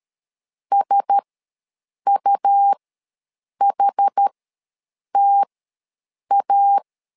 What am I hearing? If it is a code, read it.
Morse code, heard as a keyed tone, about 13 wpm, spaced as for 7 wpm, 800 Hz. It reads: SUHTA
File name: LEZIONI / 4 Q1S3 ws 12wpm 2038